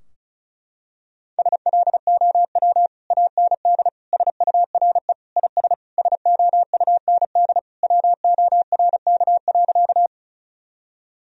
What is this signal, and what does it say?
Morse code, heard as a keyed tone, about 34 wpm, 700 Hz. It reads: SLOW AND SURE IS SOUND WORK.
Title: Sentences from Top 1000 Words - Review Part 2 - 35wpm